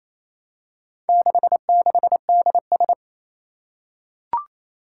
Morse code, heard as a keyed tone, about 27 wpm, 700 Hz. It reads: 66DS
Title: Four Challenging Letter-Number Combinations 28wpm